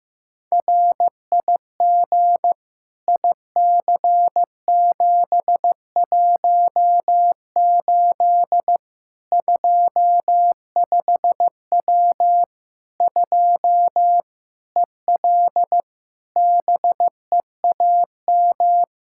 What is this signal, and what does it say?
Morse code, heard as a keyed tone, about 15 wpm, 700 Hz. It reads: RIG IC718 25W 2 EL BEAM
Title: Practice QSO - Exchange 2 - Info - 15wpm